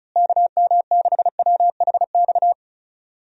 Morse code, heard as a keyed tone, about 34 wpm, 700 Hz. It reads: KM6WHX
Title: USA Call Signs - 6 Characters - Repeated 3x - 35wpm